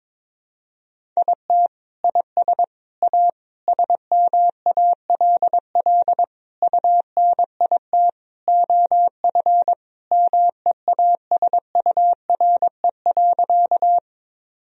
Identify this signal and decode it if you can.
Morse code, heard as a keyed tone, about 22 wpm, 700 Hz. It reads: IT IS A SMALL UNIT OF MEASURE.
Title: Sentences from Top 1000 Words - Review Part 2 - 22wpm